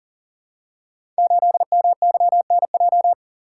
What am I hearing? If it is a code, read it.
Morse code, heard as a keyed tone, about 38 wpm, 690 Hz. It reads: 8MYNJ
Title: Code Groups - 5 Letter:Numbers - 40wpm